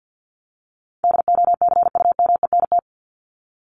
Morse code, heard as a keyed tone, about 50 wpm, 700 Hz. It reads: DOCUMENT
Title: Top 1751-2000 Words 50wpm